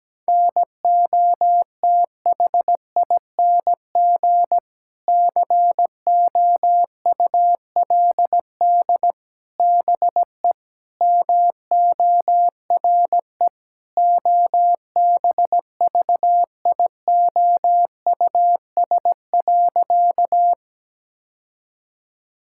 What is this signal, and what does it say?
Morse code, heard as a keyed tone, about 17 wpm, 700 Hz. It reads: NOTHING COULD BE MORE OBVIOUS.
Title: Sentences from Top 1600 Words 17wpm